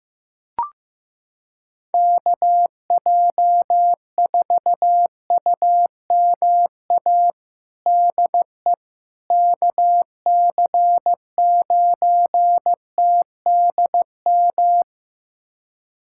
Morse code, heard as a keyed tone, about 15 wpm, 700 Hz. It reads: KJ4UMA DE KC9TDM
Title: USA Callsign DE Callsign - 6 Characters - 15wpm